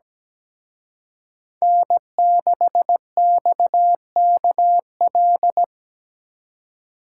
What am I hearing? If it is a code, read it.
Morse code, heard as a keyed tone, about 20 wpm, 700 Hz. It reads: N6XKL
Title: Speed Racing - USA 1x3 Call Signs 17wpm